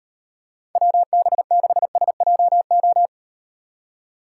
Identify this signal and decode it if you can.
Morse code, heard as a keyed tone, about 38 wpm, 700 Hz. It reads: WB6SJO